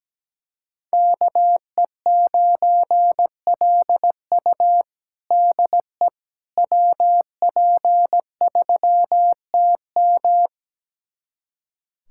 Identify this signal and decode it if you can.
Morse code, heard as a keyed tone, about 17 wpm, 700 Hz. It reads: KE9LU DE WP3TM
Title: USA Callsign DE Callsign - 5 Characters - 17wpm